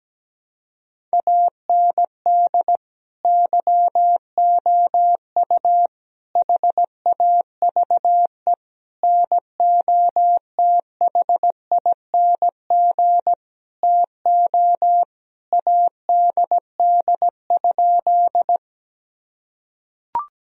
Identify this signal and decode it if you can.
Morse code, heard as a keyed tone, about 17 wpm, 700 Hz. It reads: AND YOU HAVE NOTHING TO ADD?
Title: Sentences from Top 1000 Words - Review Part 1 - 17wpm